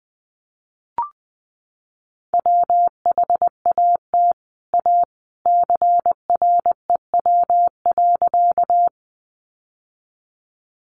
Morse code, heard as a keyed tone, about 20 wpm, 700 Hz. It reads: WHAT A CREW.
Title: Sentences from Top 1500 Words 20wpm